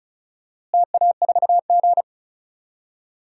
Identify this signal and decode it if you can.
Morse code, heard as a keyed tone, about 34 wpm, 700 Hz. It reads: TA4G
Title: DXCC-100 Call Signs - 4 Characters - Repeated 2x - 35wpm